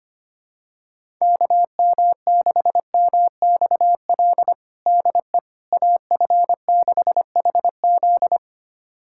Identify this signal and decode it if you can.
Morse code, heard as a keyed tone, about 25 wpm, 700 Hz. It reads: KM6MXL DE AF6HZ
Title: USA Callsign DE Callsign 25wpm